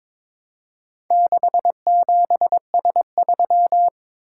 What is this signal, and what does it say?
Morse code, heard as a keyed tone, about 21 wpm, 700 Hz. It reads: 67S3
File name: practice / 22wpm / Four Letter-Number Combinations - 22wpm